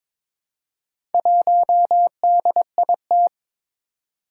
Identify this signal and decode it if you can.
Morse code, heard as a keyed tone, about 22 wpm, 700 Hz. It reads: 1DIT
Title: Four Challenging Letter-Number Combinations 22wpm